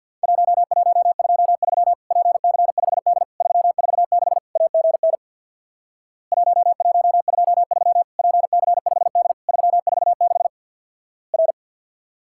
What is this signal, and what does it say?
Morse code, heard as a keyed tone, about 50 wpm, 700 Hz. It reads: 1123 PC5D 346 AGN 1123 PC5D 346 R
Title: Worked All Europe DX Contest - QTC Exchange with Repeats - 50wpm